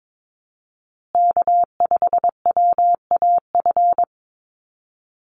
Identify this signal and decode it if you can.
Morse code, heard as a keyed tone, about 22 wpm, 700 Hz. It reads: K5WAF
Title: Call Signs Repeated 2x - 22wpm